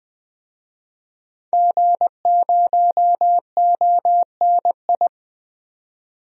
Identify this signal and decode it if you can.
Morse code, heard as a keyed tone, about 20 wpm, 700 Hz. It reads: G0ONI
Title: DXCC-100 Call Signs - 5 Characters - 20wpm